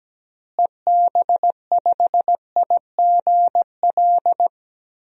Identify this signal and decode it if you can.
Morse code, heard as a keyed tone, about 17 wpm, 700 Hz. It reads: EB5IGL
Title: DXCC-100 Call Signs - 6 Characters - Repeated 3x - 17wpm